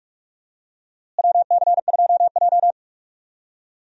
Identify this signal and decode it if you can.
Morse code, heard as a keyed tone, about 46 wpm, 700 Hz. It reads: WC2J
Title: USA Call Signs - 4 Characters - Repeated 3x - 45wpm